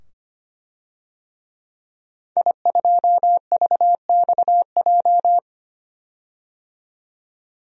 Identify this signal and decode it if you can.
Morse code, heard as a keyed tone, about 25 wpm, 700 Hz. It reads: I2VXJ
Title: DXCC-100 Call Signs - 5 Characters - 25wpm